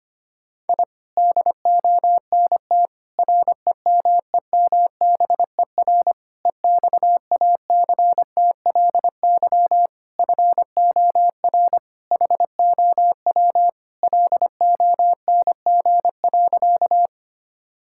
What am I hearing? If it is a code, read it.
Morse code, heard as a keyed tone, about 25 wpm, 700 Hz. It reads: I DONT REMEMBER EXACTLY FOR HOW LONG.
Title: Sentences from Top 1000 Words - Review Part 6 - 25wpm